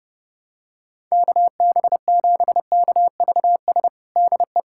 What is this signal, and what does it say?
Morse code, heard as a keyed tone, about 30 wpm, 700 Hz. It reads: KB7KVS DE
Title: USA Callsign DE Callsign 30wpm